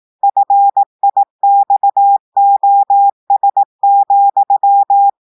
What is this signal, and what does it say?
Morse code, heard as a keyed tone, about 18 wpm, 800 Hz. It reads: FIXOS,
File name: portugais texte en clair 2 vitesse 18 mots minutes